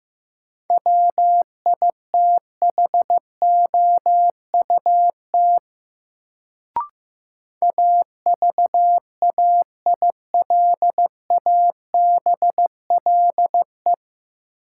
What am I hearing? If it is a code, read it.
Morse code, heard as a keyed tone, about 15 wpm, 700 Hz. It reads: WITHOUT AVAILABLE
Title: Sentences from Top 600 Words - New Word Drill - 15wpm